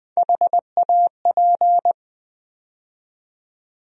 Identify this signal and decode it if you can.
Morse code, heard as a keyed tone, about 20 wpm, 700 Hz. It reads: HAP
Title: USA Call Sign Postfixes - 3 Letters 20wpm